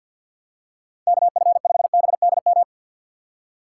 Morse code, heard as a keyed tone, about 50 wpm, 700 Hz. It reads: KF5BDM
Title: USA Call Signs - 6 Characters - 3x Character Spacing - 50wpm